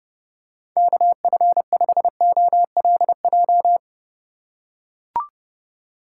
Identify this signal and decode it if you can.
Morse code, heard as a keyed tone, about 30 wpm, 700 Hz. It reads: KF5OLJ
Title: USA Call Signs - 6 Characters - 4x Character Spacing - 30wpm